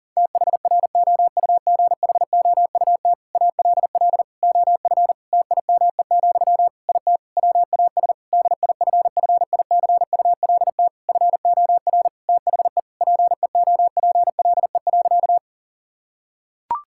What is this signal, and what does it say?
Morse code, heard as a keyed tone, about 40 wpm, 700 Hz. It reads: THROUGHOUT ALL OF TIME, IT WAS DIFFICULT FOR THE PEOPLE.
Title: Sentences from Top 1000 Words - Review Part 5 - 40wpm